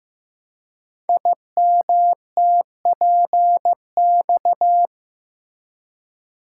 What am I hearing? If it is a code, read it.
Morse code, heard as a keyed tone, about 15 wpm, 700 Hz. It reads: IMTPX
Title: Code Groups - 5 Letters - 15wpm